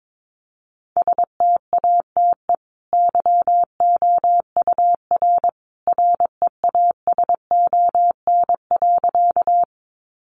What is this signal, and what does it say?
Morse code, heard as a keyed tone, about 22 wpm, 700 Hz. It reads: STATE YOUR REASON.
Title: Sentences from Top 700 Words 22wpm